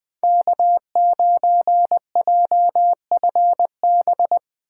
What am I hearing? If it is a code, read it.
Morse code, heard as a keyed tone, about 20 wpm, 700 Hz. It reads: K9JFB
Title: 2023 POTA Activator Call Signs - 20wpm